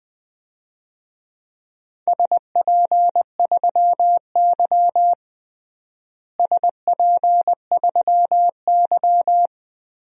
Morse code, heard as a keyed tone, about 20 wpm, 700 Hz. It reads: SP3Y SP3Y
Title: DXCC-100 Call Signs - 4 Characters - Repeated 3x - 20wpm